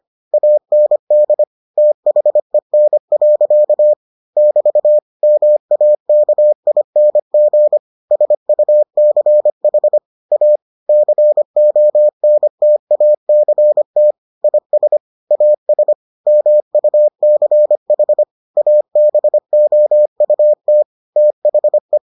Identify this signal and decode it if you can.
Morse code, heard as a keyed tone, about 25 wpm, 600 Hz. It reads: AND THEN. <BT> MAKING SUCH A CONTACT IS AS MUCH ABOUT THE